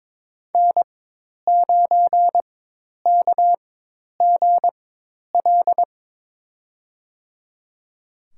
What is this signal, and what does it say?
Morse code, heard as a keyed tone, about 22 wpm, spaced as for 10 wpm, 700 Hz. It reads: N9KGL
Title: USA Call Signs - 5 Characters - 4x Character Spacing - 22wpm